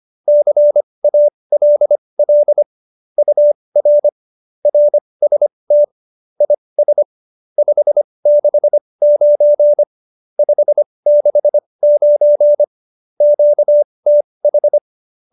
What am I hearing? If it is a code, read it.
Morse code, heard as a keyed tone, about 25 wpm, spaced as for 20 wpm, 590 Hz. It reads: CALL UR RST IS 569 569 QTH
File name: qso208b 20